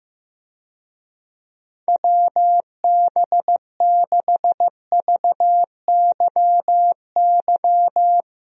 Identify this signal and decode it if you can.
Morse code, heard as a keyed tone, about 15 wpm, 700 Hz. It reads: WB6VYY
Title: USA Call Signs - 6 Characters - Repeated 2x - 15wpm